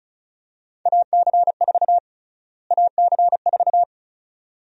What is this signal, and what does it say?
Morse code, heard as a keyed tone, about 34 wpm, 700 Hz. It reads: AC4 AC4
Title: USA Call Sign Prefixes - Repeated 2x - 35wpm